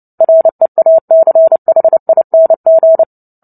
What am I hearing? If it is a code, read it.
Morse code, heard as a keyed tone, about 30 wpm, 650 Hz. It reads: REACHING
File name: Book Two / 03-Chapter 3 - 29 WPM